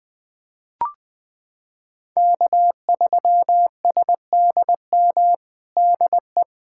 Morse code, heard as a keyed tone, about 20 wpm, 700 Hz. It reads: K3SDM DE
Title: POTA QSO Practice - 20wpm